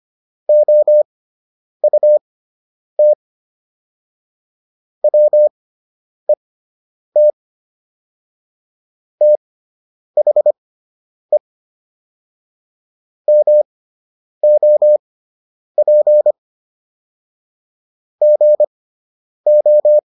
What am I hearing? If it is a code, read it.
Morse code, heard as a keyed tone, about 25 wpm, spaced as for 9 wpm, 600 Hz. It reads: OUT WET THE MOP GO